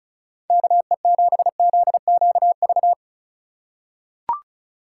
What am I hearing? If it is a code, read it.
Morse code, heard as a keyed tone, about 36 wpm, 690 Hz. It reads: KE7ZQV
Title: USA Call Signs - 6 Characters - 6x Character Spacing - 35wpm